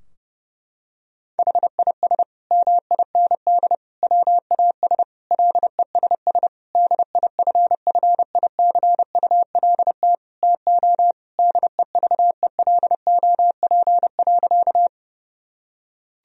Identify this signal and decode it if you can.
Morse code, heard as a keyed tone, about 30 wpm, 700 Hz. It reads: HIS MIND WAS LESS DIFFICULT TO DEVELOP.
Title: Sentences from Top 1000 Words - Review Part 2 - 30wpm